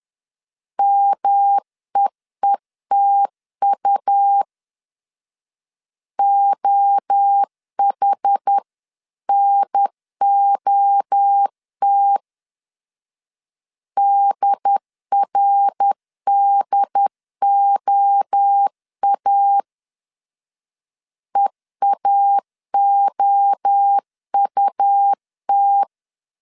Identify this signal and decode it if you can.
Morse code, heard as a keyed tone, about 10 wpm, 800 Hz. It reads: MEETU OHNOT DRDOA EAOUT
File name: 10wpm